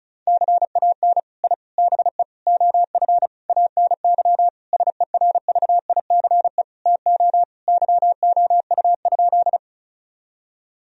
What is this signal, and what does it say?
Morse code, heard as a keyed tone, about 36 wpm, 700 Hz. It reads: CAN I BE OF ANY SERVICE TO YOU?
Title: Sentences from Top 1000 Words - Review Part 4 - 35wpm